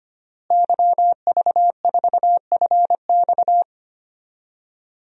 Morse code, heard as a keyed tone, about 25 wpm, 700 Hz. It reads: YV4FX
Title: DXCC-100 Call Signs - 5 Characters - Repeated 3x - 25wpm